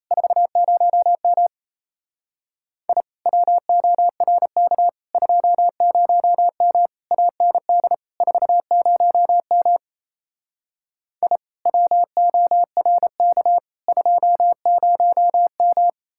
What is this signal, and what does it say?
Morse code, heard as a keyed tone, about 38 wpm, 700 Hz. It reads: 40M I WORK 20M AND 40M I WORK 20M